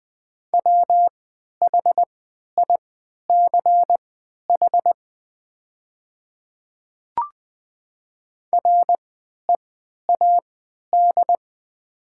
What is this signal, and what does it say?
Morse code, heard as a keyed tone, about 20 wpm, spaced as for 11 wpm, 700 Hz. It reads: WHICH READ
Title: 3x Character Spacing - Top 300 Words - 20wpm